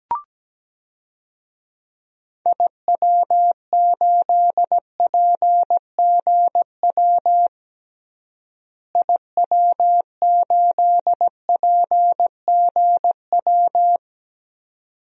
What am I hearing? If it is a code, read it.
Morse code, heard as a keyed tone, about 17 wpm, 700 Hz. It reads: IW8PGW IW8PGW
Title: DXCC-100 Call Signs - 6 Characters - Repeated 2x - 17wpm